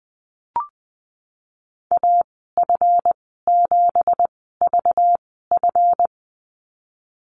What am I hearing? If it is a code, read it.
Morse code, heard as a keyed tone, about 20 wpm, spaced as for 15 wpm, 700 Hz. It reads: AF7VF
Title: 193 Lesson 31g - Letter v - Call Signs 20wpm